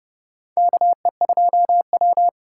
Morse code, heard as a keyed tone, about 31 wpm, 700 Hz. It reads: KE2W
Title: USA Call Signs - 4 Characters - 3x Character Spacing - 30wpm